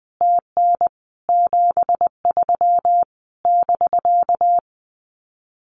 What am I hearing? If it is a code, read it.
Morse code, heard as a keyed tone, about 20 wpm, 700 Hz. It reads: TN 73 <BK>
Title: Common QSO Phrases Encore 20wpm